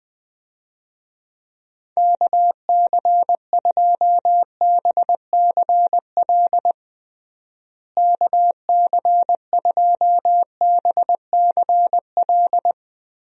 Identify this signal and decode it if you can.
Morse code, heard as a keyed tone, about 21 wpm, 700 Hz. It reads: KC2BCL KC2BCL
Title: USA Call Signs - 6 Characters - Repeated 2x - 20wpm